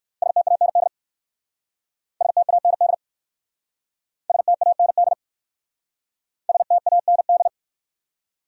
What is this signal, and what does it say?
Morse code, heard as a keyed tone, about 58 wpm, 700 Hz. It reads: STAND STAND STAND STAND